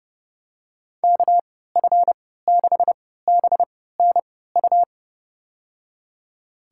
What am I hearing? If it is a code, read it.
Morse code, heard as a keyed tone, about 30 wpm, spaced as for 17 wpm, 700 Hz. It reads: KF6BNU
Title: USA Call Signs - 3x Character Spacing - 30wpm